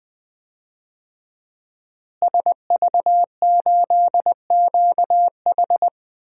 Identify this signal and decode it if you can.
Morse code, heard as a keyed tone, about 20 wpm, 700 Hz. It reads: SV8QH
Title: DXCC-100 Call Signs - 5 Characters - Repeated 2x - 20wpm